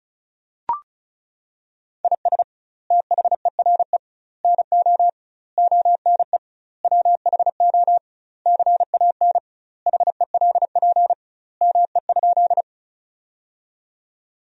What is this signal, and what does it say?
Morse code, heard as a keyed tone, about 34 wpm, 700 Hz. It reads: IS THERE NO ONE WHO CAN HELP ME?
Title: 2x Word-Spacing - Sentences from Top 200 Words - 35wpm